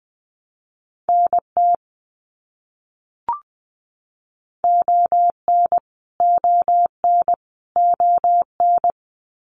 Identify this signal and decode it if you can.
Morse code, heard as a keyed tone, about 20 wpm, 700 Hz. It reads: NT ON ON ON
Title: Canadian Provinces and Territories - Repeated 3x - 20wpm